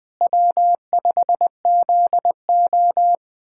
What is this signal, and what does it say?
Morse code, heard as a keyed tone, about 19 wpm, 700 Hz. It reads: W5ZO
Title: K1USN SST QSO Practice - 20wpm